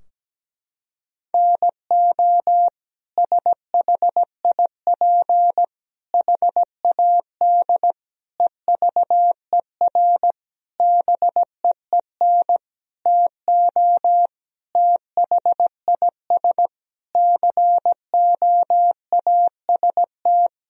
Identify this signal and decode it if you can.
Morse code, heard as a keyed tone, about 17 wpm, 700 Hz. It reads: NO SHIP HAD EVER BEEN TO THIS COAST
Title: Sentences from Top 1000 Words - Review Part 4 - 17wpm